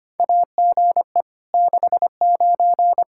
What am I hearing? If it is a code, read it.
Morse code, heard as a keyed tone, about 25 wpm, 700 Hz. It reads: AGE 69
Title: Practice QSO - Exchange 3 - More Info - 25wpm